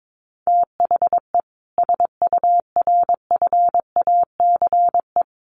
Morse code, heard as a keyed tone, about 22 wpm, 700 Hz. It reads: THE SURFACE
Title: Sets of 4 Words 22wpm